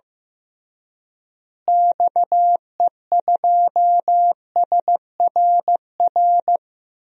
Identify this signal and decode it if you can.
Morse code, heard as a keyed tone, about 15 wpm, 700 Hz. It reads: XE2SRR
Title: DXCC-100 Call Signs - 6 Characters - Repeated 3x - 15wpm